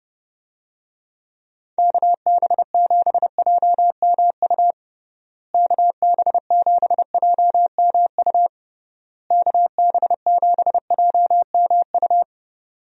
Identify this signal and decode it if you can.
Morse code, heard as a keyed tone, about 30 wpm, 700 Hz. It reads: KB7JMU KB7JMU KB7JMU
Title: USA Call Signs - 6 Characters - Repeated 3x - 30wpm